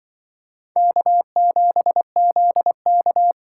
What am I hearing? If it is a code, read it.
Morse code, heard as a keyed tone, about 23 wpm, 700 Hz. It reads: K7ZK
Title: Speed Racing - USA 1x2 Call Signs 20wpm